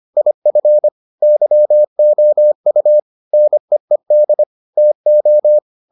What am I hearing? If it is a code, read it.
Morse code, heard as a keyed tone, about 25 wpm, 600 Hz. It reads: IF YOU NEED TO